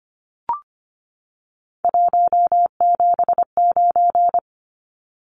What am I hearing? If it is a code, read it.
Morse code, heard as a keyed tone, about 25 wpm, 700 Hz. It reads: 179
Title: Numbers - 3-Digit 25wpm